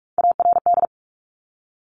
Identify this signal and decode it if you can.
Morse code, heard as a keyed tone, about 46 wpm, 700 Hz. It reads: ARD
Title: Top Three Letter Combinations in Words 45wpm